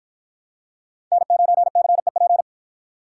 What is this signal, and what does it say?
Morse code, heard as a keyed tone, about 53 wpm, 700 Hz. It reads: N9CEP